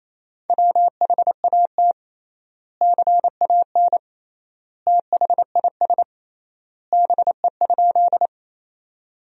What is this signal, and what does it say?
Morse code, heard as a keyed tone, about 28 wpm, 700 Hz. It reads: WHAT CAN THIS BE?
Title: 3x Word-Spacing - Sentences from Top 100 Words - 28wpm